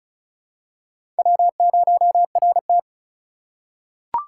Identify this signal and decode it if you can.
Morse code, heard as a keyed tone, about 36 wpm, 700 Hz. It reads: W0RT
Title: USA Call Signs - 4 Characters - 4x Character Spacing - 35wpm